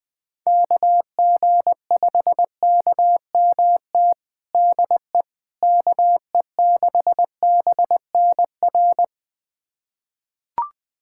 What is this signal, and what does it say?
Morse code, heard as a keyed tone, about 20 wpm, 700 Hz. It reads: KG5KMT DE KE6BNR
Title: USA Callsign DE Callsign - 6 Characters - 20wpm